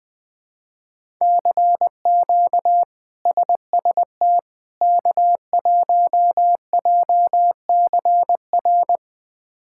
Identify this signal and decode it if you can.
Morse code, heard as a keyed tone, about 21 wpm, 700 Hz. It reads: CQ SST K1JCR
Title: K1USN SST QSO Practice - 20wpm